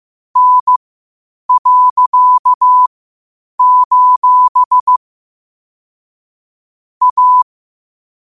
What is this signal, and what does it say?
Morse code, heard as a keyed tone, about 15 wpm, spaced as for 8 wpm, 1000 Hz. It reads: N.: A
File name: qotdfiles / 8wpm